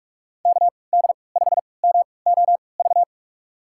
Morse code, heard as a keyed tone, about 44 wpm, spaced as for 25 wpm, 700 Hz. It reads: KD5MOV
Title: USA Call Signs - 6 Characters - 3x Character Spacing - 45wpm